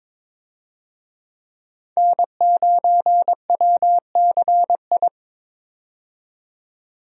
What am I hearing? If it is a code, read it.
Morse code, heard as a keyed tone, about 21 wpm, 700 Hz. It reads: N9WCI
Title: USA Call Signs - 5 Characters - 22wpm